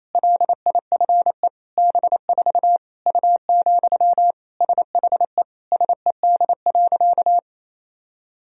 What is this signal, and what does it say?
Morse code, heard as a keyed tone, about 27 wpm, 700 Hz. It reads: LIFE B4 U, SHE SED.
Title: Sentences from Top 400 Words and Abbreviations 28wpm